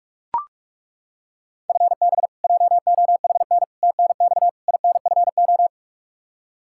Morse code, heard as a keyed tone, about 42 wpm, 700 Hz. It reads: FB JOHN TNX INFO
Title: Practice QSO - Exchange 3 - More Info - 45wpm